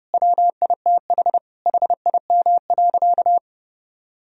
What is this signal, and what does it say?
Morse code, heard as a keyed tone, about 30 wpm, 700 Hz. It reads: WITH HIM.
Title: Sentences from Top 1000 Words - Review Part 2 - 30wpm